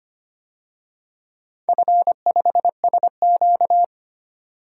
Morse code, heard as a keyed tone, about 25 wpm, 700 Hz. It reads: F5SQ
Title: DXCC-100 Call Signs - 4 Characters - 25wpm